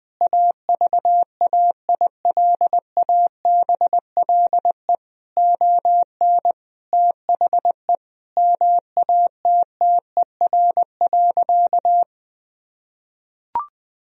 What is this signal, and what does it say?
Morse code, heard as a keyed tone, about 20 wpm, 700 Hz. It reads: AVAILABLE ON THE MATTER.
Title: Sentences from Top 1000 Words - Review Part 3 - 20wpm